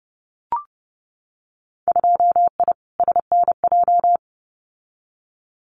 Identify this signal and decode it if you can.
Morse code, heard as a keyed tone, about 30 wpm, 700 Hz. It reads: 2I SNJ
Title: Winter Field Day Exchanges Encore 30wpm